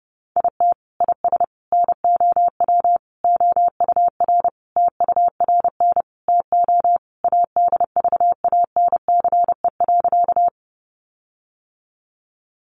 Morse code, heard as a keyed tone, about 30 wpm, 700 Hz. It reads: IT IS NOW OUR TURN TO ADVANCE.
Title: Sentences from Top 1400 Words 30wpm